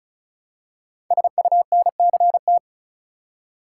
Morse code, heard as a keyed tone, about 36 wpm, 700 Hz. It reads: SUNCT
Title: Code Groups - 5 Letter:Numbers - 35wpm